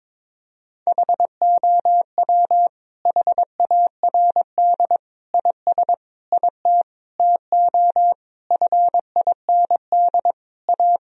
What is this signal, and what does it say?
Morse code, heard as a keyed tone, about 22 wpm, 700 Hz. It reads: HOW HARD IS IT TO FIND A
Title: Sentences from Top 1000 Words - Review Part 1 - 22wpm